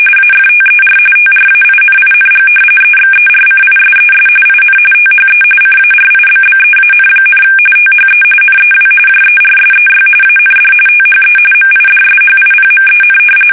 RTTY, about 45.45 baud. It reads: AFTER TRAINING EXERCISES IN THE CARIBBEAN, THE NEW JERSEY AND THE MIDSHIPMEN
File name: fsk850 / battleshipnj